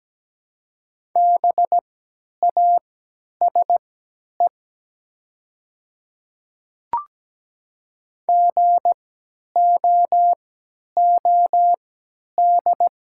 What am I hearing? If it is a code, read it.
Morse code, heard as a keyed tone, about 17 wpm, spaced as for 10 wpm, 700 Hz. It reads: BASE GOOD
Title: 3x Character Spacing - Top 300 Words - 17wpm